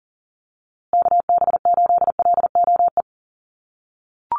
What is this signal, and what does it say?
Morse code, heard as a keyed tone, about 40 wpm, 700 Hz. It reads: KB8LOE
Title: USA Call Signs Repeated 2x 40wpm